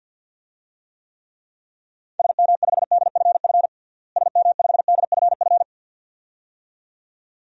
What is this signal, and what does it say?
Morse code, heard as a keyed tone, about 53 wpm, 700 Hz. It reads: SM5DFF SM5DFF
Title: DXCC-100 Call Signs - 6 Characters - Repeated 2x - 50wpm